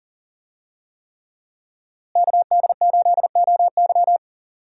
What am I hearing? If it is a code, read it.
Morse code, heard as a keyed tone, about 40 wpm, 690 Hz. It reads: KD8OY
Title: USA Call Signs - 5 Characters - Repeated 3x - 40wpm